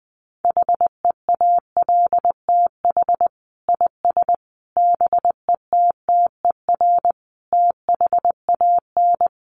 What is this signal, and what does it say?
Morse code, heard as a keyed tone, about 20 wpm, 700 Hz. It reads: HEALTH IS BETTER THAN
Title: Sentences from Top 1200 Words 20wpm